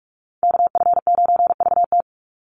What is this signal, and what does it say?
Morse code, heard as a keyed tone, about 44 wpm, 700 Hz. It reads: KF9VT
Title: USA Call Signs 45wpm